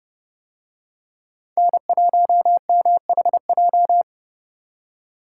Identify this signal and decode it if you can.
Morse code, heard as a keyed tone, about 29 wpm, 700 Hz. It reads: N1MHJ